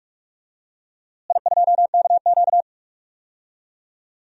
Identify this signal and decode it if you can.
Morse code, heard as a keyed tone, about 46 wpm, 700 Hz. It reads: I2KQ
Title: DXCC-100 Call Signs - 4 Characters - Repeated 3x - 45wpm